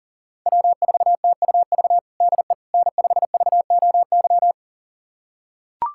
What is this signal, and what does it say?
Morse code, heard as a keyed tone, about 40 wpm, 700 Hz. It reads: W4TUV DE N5VOY
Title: USA Callsign DE Callsign - 5 Characters - 40wpm